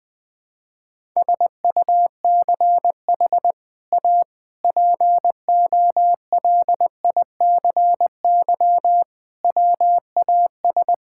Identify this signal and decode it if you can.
Morse code, heard as a keyed tone, about 21 wpm, 700 Hz. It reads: SUCH A POLICY WAS
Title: Sentences from Top 1000 Words - Review Part 5 - 20wpm